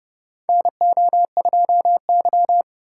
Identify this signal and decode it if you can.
Morse code, heard as a keyed tone, about 30 wpm, 700 Hz. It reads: NO2Y
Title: USA Call Signs - 4 Characters - 4x Character Spacing - 30wpm